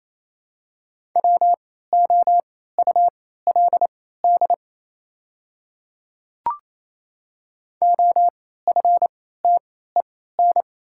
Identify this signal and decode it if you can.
Morse code, heard as a keyed tone, about 28 wpm, spaced as for 16 wpm, 700 Hz. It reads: WOULD OFTEN
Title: 3x Character Spacing - Top 300 Words - 28wpm